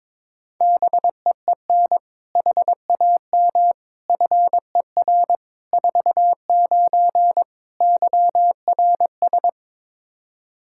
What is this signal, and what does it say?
Morse code, heard as a keyed tone, about 22 wpm, 700 Hz. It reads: BEEN HAM FER 49 YRS
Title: Practice QSO - Exchange 3 - More Info - 22wpm